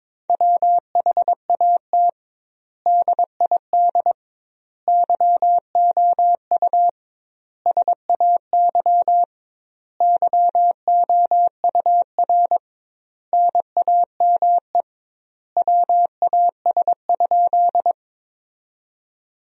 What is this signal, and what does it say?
Morse code, heard as a keyed tone, about 21 wpm, 700 Hz. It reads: WHAT DID YOU SAY YOUR NAME WAS?
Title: 2x Word-Spacing - Sentences from Top 200 Words - 22wpm